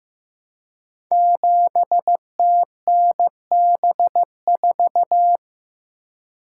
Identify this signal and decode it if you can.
Morse code, heard as a keyed tone, about 15 wpm, 700 Hz. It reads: 7TNB4
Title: Code Groups - 5 Letter:Numbers - 15wpm